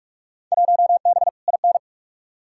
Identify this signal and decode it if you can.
Morse code, heard as a keyed tone, about 46 wpm, 700 Hz. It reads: 1B IN
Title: Field Day Exchanges - 45wpm